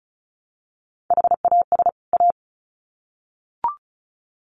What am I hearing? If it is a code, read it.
Morse code, heard as a keyed tone, about 34 wpm, 700 Hz. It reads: HAS A
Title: Sets of 2 Words 35wpm